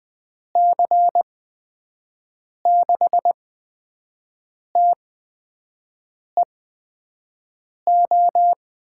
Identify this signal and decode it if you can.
Morse code, heard as a keyed tone, about 20 wpm, spaced as for 5 wpm, 700 Hz. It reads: C6TEO
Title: USA Call Signs - 6 Characters - 8x Character Spacing - 20wpm